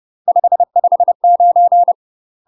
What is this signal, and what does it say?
Morse code, heard as a keyed tone, about 30 wpm, 700 Hz. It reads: 559
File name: UK MASTER QSOs TRAINING / 30WPM / UK MASTER QSOs 0001